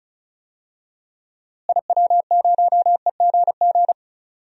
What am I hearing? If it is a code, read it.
Morse code, heard as a keyed tone, about 34 wpm, 700 Hz. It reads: IW0EGG
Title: DXCC-100 Call Signs - 6 Characters - Repeated 2x - 35wpm